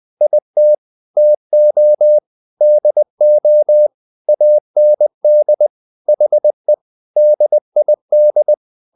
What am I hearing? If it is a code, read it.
Morse code, heard as a keyed tone, about 20 wpm, 600 Hz. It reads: IT TO DO AND HE DID
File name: the old man and the sea part 2 20wpm